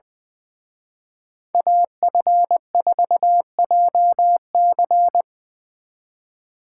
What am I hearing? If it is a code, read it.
Morse code, heard as a keyed tone, about 23 wpm, 700 Hz. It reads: AF4JC
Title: Speed Racing - USA 2x2 Call Signs 20wpm